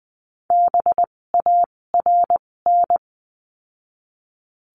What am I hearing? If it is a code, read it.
Morse code, heard as a keyed tone, about 20 wpm, spaced as for 16 wpm, 700 Hz. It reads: BARN
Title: 256 Lesson 42a - Words 12-20wpm